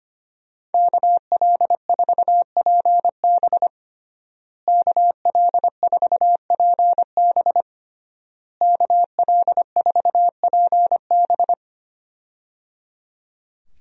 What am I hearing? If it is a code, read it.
Morse code, heard as a keyed tone, about 25 wpm, 700 Hz. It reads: KL4PB KL4PB KL4PB
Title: USA Call Signs - 5 Characters - Repeated 3x - 25wpm